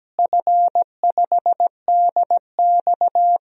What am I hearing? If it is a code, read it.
Morse code, heard as a keyed tone, about 17 wpm, 700 Hz. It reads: F5DX
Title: DXCC-100 Call Signs - 4 Characters - Repeated 3x - 17wpm